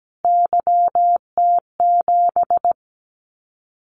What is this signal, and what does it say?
Morse code, heard as a keyed tone, about 17 wpm, 700 Hz. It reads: YT7
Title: Callsign Prefixes for Top 25 DXCC - Repeated 3x - 17wpm